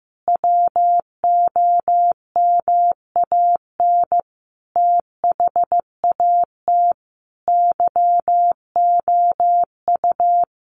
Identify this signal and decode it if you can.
Morse code, heard as a keyed tone, about 15 wpm, 700 Hz. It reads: WOMAN THAT YOU
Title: Sentences from Top 1100 Words 15wpm